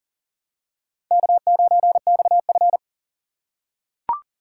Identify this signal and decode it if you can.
Morse code, heard as a keyed tone, about 40 wpm, 700 Hz. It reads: K9XF
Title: USA Call Signs - 4 Characters - 4x Character Spacing - 40wpm